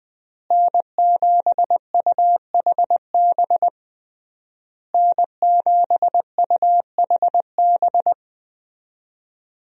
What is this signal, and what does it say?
Morse code, heard as a keyed tone, about 20 wpm, 700 Hz. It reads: N7UHB N7UHB
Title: USA Call Signs - 5 Characters - Repeated 3x - 20wpm